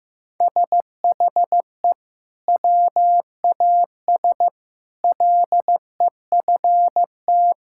Morse code, heard as a keyed tone, about 15 wpm, 700 Hz. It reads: SHE WAS LEFT
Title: Sentences from Top 1600 Words 15wpm